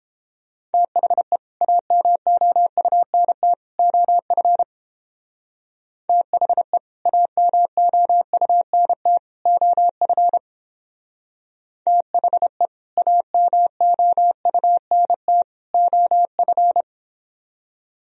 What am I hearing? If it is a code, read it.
Morse code, heard as a keyed tone, about 29 wpm, 700 Hz. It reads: THE AMOUNT OF THE AMOUNT OF THE AMOUNT OF